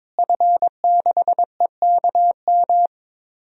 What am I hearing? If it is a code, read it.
Morse code, heard as a keyed tone, about 22 wpm, 700 Hz. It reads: F6EKM
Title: DXCC-100 Call Signs - 5 Characters - Repeated 2x - 22wpm